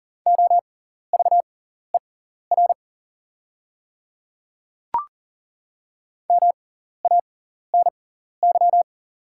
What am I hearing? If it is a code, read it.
Morse code, heard as a keyed tone, about 39 wpm, spaced as for 14 wpm, 700 Hz. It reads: OVER MANY
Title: 6x Character Spacing - Top 100 Words - 40wpm